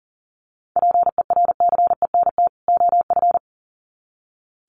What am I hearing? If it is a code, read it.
Morse code, heard as a keyed tone, about 37 wpm, 700 Hz. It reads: PERCENT OF
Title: Sets of 2 Words 40wpm